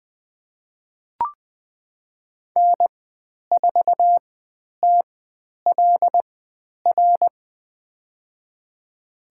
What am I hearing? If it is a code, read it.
Morse code, heard as a keyed tone, about 20 wpm, spaced as for 10 wpm, 700 Hz. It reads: N4TLR
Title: 102 Lesson 17g - Letter u - Call Signs 20wpm